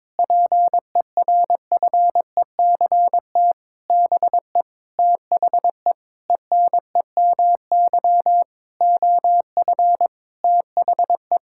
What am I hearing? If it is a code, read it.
Morse code, heard as a keyed tone, about 22 wpm, 700 Hz. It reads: PERFECT BE THE ENEMY OF THE
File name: Sentences with 9-Word Phrases - 22wpm